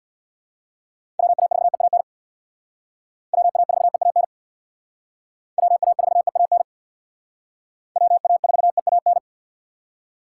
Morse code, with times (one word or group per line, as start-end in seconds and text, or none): 1.2-2.0 WA4EAN
3.3-4.2 WA4EAN
5.6-6.6 WA4EAN
8.0-9.2 WA4EAN